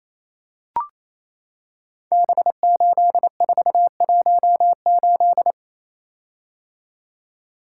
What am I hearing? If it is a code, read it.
Morse code, heard as a keyed tone, about 28 wpm, 700 Hz. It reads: B8418
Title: Code Groups - 5 Letter:Numbers - 28wpm